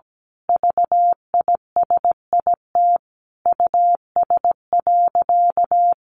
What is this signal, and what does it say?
Morse code, heard as a keyed tone, about 17 wpm, 700 Hz. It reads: VISIT US.
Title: Sentences from Top 1400 Words 17wpm